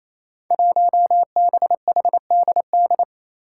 Morse code, heard as a keyed tone, about 28 wpm, 700 Hz. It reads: 1BHDD
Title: Five Challenging Letter-Number Combinations 28wpm